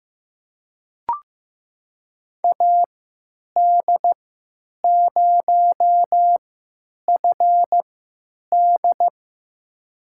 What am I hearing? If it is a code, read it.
Morse code, heard as a keyed tone, about 15 wpm, spaced as for 9 wpm, 700 Hz. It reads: AD0FD
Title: USA Call Signs - 5 Characters - 3x Character Spacing - 15wpm